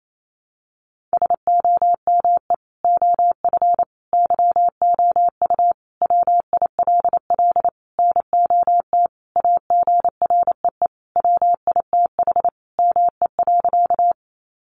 Morse code, heard as a keyed tone, about 28 wpm, 700 Hz. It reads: SOME OF YOU WILL NOT AGREE WITH ME.
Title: Sentences from Top 1100 Words 28wpm